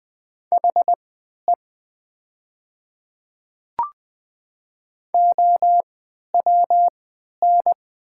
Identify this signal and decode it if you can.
Morse code, heard as a keyed tone, about 20 wpm, spaced as for 11 wpm, 700 Hz. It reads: HE OWN
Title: 3x Character Spacing - Top 500 Words - 20wpm